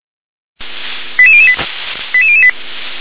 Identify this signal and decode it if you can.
Five-tone selective call (ZVEI), heard as quick runs of five tones, tones of about 70 ms. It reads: 88008 88008